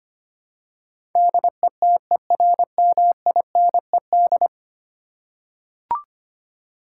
Morse code, heard as a keyed tone, about 25 wpm, 700 Hz. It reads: DETERMINED
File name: 10 Letter Words - First 3 Letters Given - 25wpm